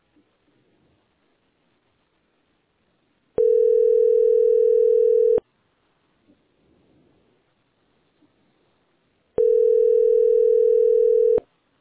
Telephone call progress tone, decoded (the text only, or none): ringback tone